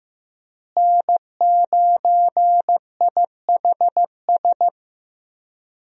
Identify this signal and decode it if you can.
Morse code, heard as a keyed tone, about 15 wpm, 700 Hz. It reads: N9IHS
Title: USA Call Signs - 5 Characters - Repeated 3x - 15wpm